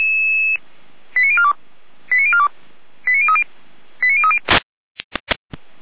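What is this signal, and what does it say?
Five-tone selective call (ZVEI), heard as quick runs of five tones, tones of about 70 ms. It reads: 89042 89042 89030 89030
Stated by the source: ZVEI five-tone call 89277 received by SDR, from the Feuerwehr Orlamünde audio server.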